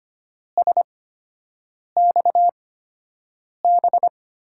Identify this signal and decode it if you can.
Morse code, heard as a keyed tone, about 25 wpm, spaced as for 7 wpm, 700 Hz. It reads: SXB
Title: USA Call Signs - 6 Characters - 8x Character Spacing - 25wpm